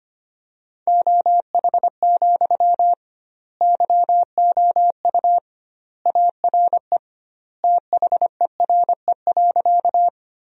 Morse code, heard as a keyed tone, about 25 wpm, 700 Hz. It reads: OH, YOU ARE THERE.
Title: 2x Word-Spacing - Sentences from Top 500 Words - 25wpm